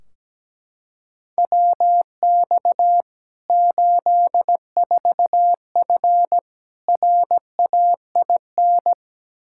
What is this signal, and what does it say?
Morse code, heard as a keyed tone, about 17 wpm, 700 Hz. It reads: WX 84F RAIN
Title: K1USN SST QSO Practice 17wpm